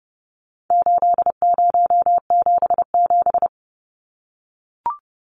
Morse code, heard as a keyed tone, about 30 wpm, 700 Hz. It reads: 8077
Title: Numbers - 4-Digit 30wpm